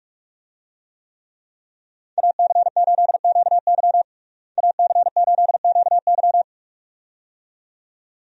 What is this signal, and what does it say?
Morse code, heard as a keyed tone, about 46 wpm, 700 Hz. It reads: AC8QY AC8QY
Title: USA Call Signs - 5 Characters - Repeated 2x - 45wpm